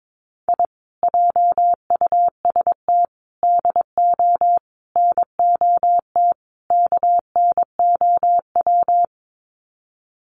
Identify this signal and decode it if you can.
Morse code, heard as a keyed tone, about 22 wpm, 700 Hz. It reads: I JUST DO NOT KNOW
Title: Sets of 5 Words Encore 22wpm